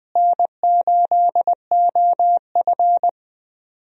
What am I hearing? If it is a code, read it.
Morse code, heard as a keyed tone, about 20 wpm, 700 Hz. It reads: N8OF
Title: USA Call Signs - 4 Characters - Repeated 3x - 20wpm